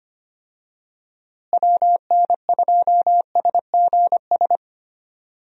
Morse code, heard as a keyed tone, about 25 wpm, 700 Hz. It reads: WN2SGS
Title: USA Call Signs - 6 Characters - Repeated 3x - 25wpm